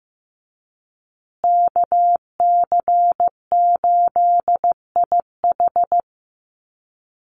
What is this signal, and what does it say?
Morse code, heard as a keyed tone, about 15 wpm, 700 Hz. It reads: KC8IH
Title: USA Call Signs Repeated 2x 15wpm